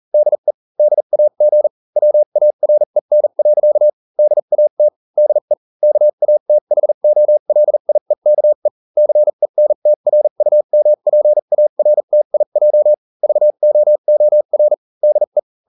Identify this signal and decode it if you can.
Morse code, heard as a keyed tone, about 40 wpm, spaced as for 35 wpm, 600 Hz. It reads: DE DAG WAREN. DAT DE KATHOLIEKE CENTRUMPARTIJ VOOR DE